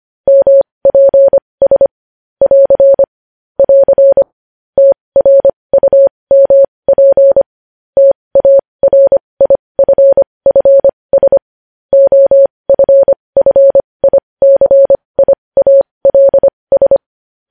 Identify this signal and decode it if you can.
Morse code, heard as a keyed tone, about 25 wpm, spaced as for 20 wpm, 550 Hz. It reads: MPS <AR> <AR> TRUMP TARIFFS OFFICIALS
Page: news headlines in morse code at 20 wpm